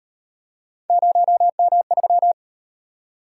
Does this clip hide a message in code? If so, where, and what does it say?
Yes: Morse code from 0.9 to 2.3 s, heard as a keyed tone, about 38 wpm, 700 Hz. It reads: 0M3